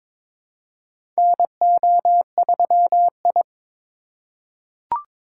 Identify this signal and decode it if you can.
Morse code, heard as a keyed tone, about 22 wpm, 700 Hz. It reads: NO3I
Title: USA Call Signs - 4 Characters - 2x Character Spacing - 22wpm